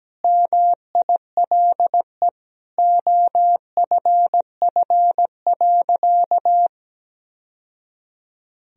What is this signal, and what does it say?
Morse code, heard as a keyed tone, about 17 wpm, 700 Hz. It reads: MILE OFF.
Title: Sentences from Top 1000 Words - Review Part 1 - 17wpm